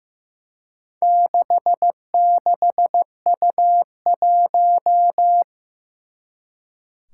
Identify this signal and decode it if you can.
Morse code, heard as a keyed tone, about 16 wpm, 700 Hz. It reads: 66U1
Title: Four Challenging Letter-Number Combinations 15wpm